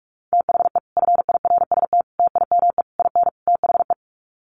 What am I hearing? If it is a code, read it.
Morse code, heard as a keyed tone, about 46 wpm, 700 Hz. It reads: THE FIRST TIME IN THE
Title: Sets of 5 Words Encore 45wpm